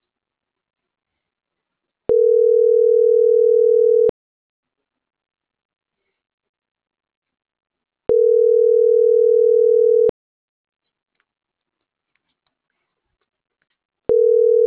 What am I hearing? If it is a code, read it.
Ringback tone (North America), 440 + 480 Hz.